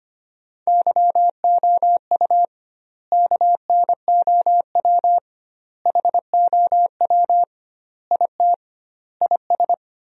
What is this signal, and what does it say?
Morse code, heard as a keyed tone, about 25 wpm, 700 Hz. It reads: YOU KNOW HOW IT IS
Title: 2x Word-Spacing - Sentences from Top 100 Words - 25wpm